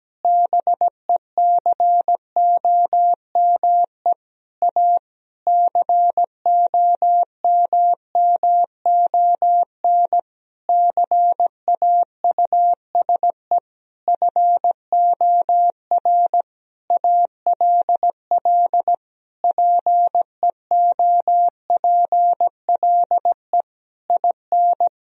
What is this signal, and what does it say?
Morse code, heard as a keyed tone, about 17 wpm, 700 Hz. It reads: BECOME A COMMON CAUSE FOR ALL PEOPLE IN
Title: Sentences from Top 1000 Words - Review Part 3 - 17wpm